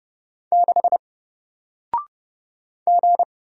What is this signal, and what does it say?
Morse code, heard as a keyed tone, about 30 wpm, 700 Hz. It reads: 6 G
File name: Single-Letter-Number-30wpm